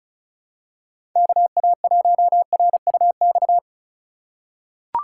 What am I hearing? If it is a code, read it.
Morse code, heard as a keyed tone, about 36 wpm, 700 Hz. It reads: KA1RUX
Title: USA Call Signs - 6 Characters - 2x Character Spacing - 35wpm